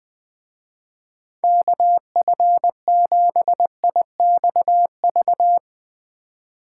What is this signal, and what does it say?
Morse code, heard as a keyed tone, about 21 wpm, 700 Hz. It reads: KF7IXV
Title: Speed Racing - USA 2x3 Call Signs 20wpm